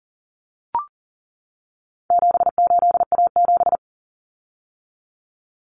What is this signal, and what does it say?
Morse code, heard as a keyed tone, about 40 wpm, 700 Hz. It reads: 78A7
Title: Common Cut Numbers - 4-Digit 40wpm